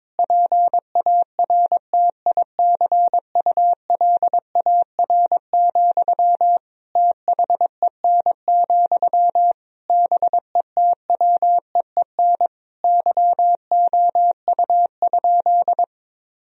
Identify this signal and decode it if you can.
Morse code, heard as a keyed tone, about 22 wpm, 700 Hz. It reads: PARTICULAR, THEN, BETWEEN YOU?
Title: Sentences from Top 1000 Words - Review Part 4 - 22wpm